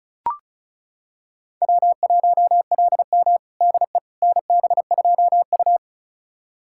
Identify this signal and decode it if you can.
Morse code, heard as a keyed tone, about 36 wpm, 700 Hz. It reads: W1LM DE NB2U
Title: USA Callsign DE Callsign - 4 Characters - 35wpm